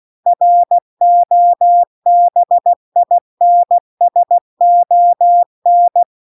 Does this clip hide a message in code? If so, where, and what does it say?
Yes: Morse code from 0.3 to 6.0 s, heard as a keyed tone, about 16 wpm, 700 Hz. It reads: ROBINSON